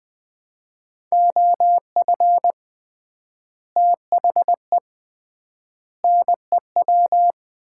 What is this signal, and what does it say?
Morse code, heard as a keyed tone, about 20 wpm, 700 Hz. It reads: OF THE NEW
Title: 3x Word-Spacing - Sets of 3 Words - 20wpm